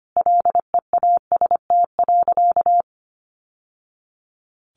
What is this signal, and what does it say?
Morse code, heard as a keyed tone, about 25 wpm, 700 Hz. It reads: LEAST.
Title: Sentences from Top 700 Words 25wpm